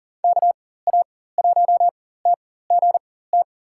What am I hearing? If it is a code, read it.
Morse code, heard as a keyed tone, about 40 wpm, spaced as for 19 wpm, 700 Hz. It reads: KA1TGT
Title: USA Call Signs - 6 Characters - 4x Character Spacing - 40wpm